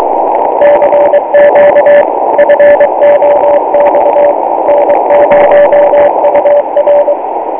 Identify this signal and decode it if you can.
Morse code, heard as a keyed tone, about 23 wpm, 580 Hz, with radio noise. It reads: CQ FOX N0UR